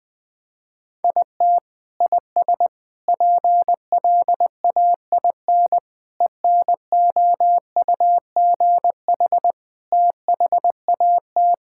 Morse code, heard as a keyed tone, about 20 wpm, 700 Hz. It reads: IT IS PLAIN ENOUGH THAT
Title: Sentences from Top 1000 Words - Review Part 5 - 20wpm